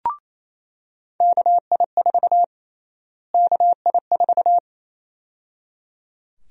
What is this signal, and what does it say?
Morse code, heard as a keyed tone, about 27 wpm, 700 Hz. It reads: KI4 KI4
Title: USA Call Sign Prefixes - Repeated 2x - 28wpm